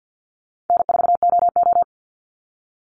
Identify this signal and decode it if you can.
Morse code, heard as a keyed tone, about 50 wpm, 700 Hz. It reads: N4OO
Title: Call Signs Repeated 2x - 50wpm